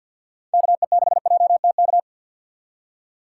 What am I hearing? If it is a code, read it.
Morse code, heard as a keyed tone, about 47 wpm, 700 Hz. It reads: KE6JTK
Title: USA Call Signs - 6 Characters - 3x Character Spacing - 50wpm